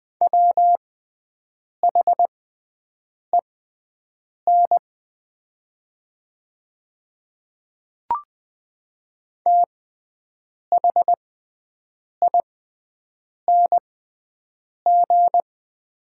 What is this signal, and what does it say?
Morse code, heard as a keyed tone, about 20 wpm, spaced as for 7 wpm, 700 Hz. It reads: WHEN THING
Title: 6x Character Spacing - Top 200 Words - 20wpm